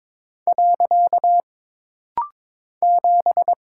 Morse code, heard as a keyed tone, about 22 wpm, 700 Hz. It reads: . 7
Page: This rapid-fire timing is geared to push your ICR (Instant Character Recognition) skills.